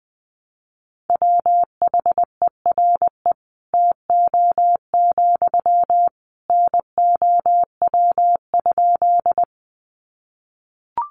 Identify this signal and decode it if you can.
Morse code, heard as a keyed tone, about 19 wpm, 700 Hz. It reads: WHERE TO, NOW?
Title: Sentences from Top 200 Words 20wpm